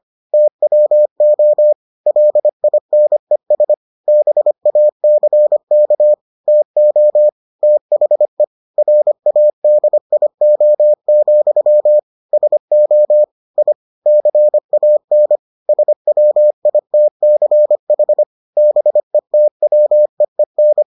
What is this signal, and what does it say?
Morse code, heard as a keyed tone, about 25 wpm, 600 Hz. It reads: TWO LINES BACK TO THE RADIO, SO I CAN SWITCH BETWEEN